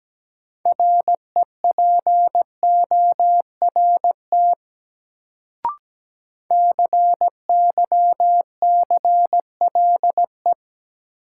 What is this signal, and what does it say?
Morse code, heard as a keyed tone, about 17 wpm, 700 Hz. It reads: REPORT CYCLE
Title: Sentences from Top 900 Words - New Word Drill - 17wpm